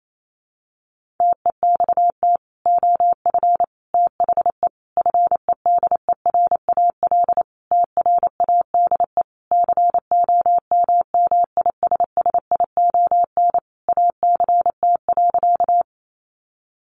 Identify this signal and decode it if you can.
Morse code, heard as a keyed tone, about 28 wpm, 700 Hz. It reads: TEXT OF THE FEDERAL TRADE COMMISSION ACT.
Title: Sentences from Top 1400 Words 28wpm